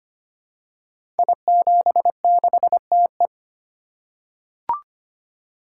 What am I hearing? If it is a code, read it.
Morse code, heard as a keyed tone, about 25 wpm, 700 Hz. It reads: I76TE
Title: Code Groups - 5 Letter:Numbers - 25wpm